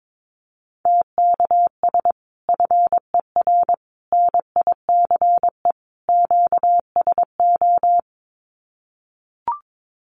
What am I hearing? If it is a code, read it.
Morse code, heard as a keyed tone, about 22 wpm, 700 Hz. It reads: TKS FER NICE QSO
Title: Common QSO Phrases 22wpm